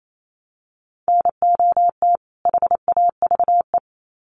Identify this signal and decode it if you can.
Morse code, heard as a keyed tone, about 28 wpm, 700 Hz. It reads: NOT HAVE
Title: Sets of 2 Words 28wpm